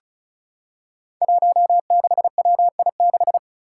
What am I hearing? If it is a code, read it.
Morse code, heard as a keyed tone, about 34 wpm, 700 Hz. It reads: 16WI6
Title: Five Challenging Letter-Number Combinations 35wpm